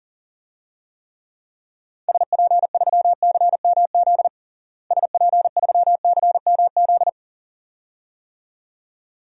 Morse code, heard as a keyed tone, about 37 wpm, 700 Hz. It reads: SP3CMZ SP3CMZ
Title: DXCC-100 Call Signs - 6 Characters - Repeated 2x - 40wpm